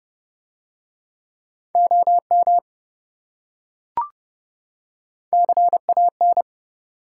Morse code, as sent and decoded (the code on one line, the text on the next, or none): --- -- / -.-. .- -.
OM CAN